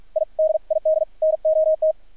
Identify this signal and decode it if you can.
Morse code, heard as a keyed tone, about 53 wpm, spaced as for 45 wpm, 630 Hz, with radio noise. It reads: I DID NOT
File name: archived / CRSnet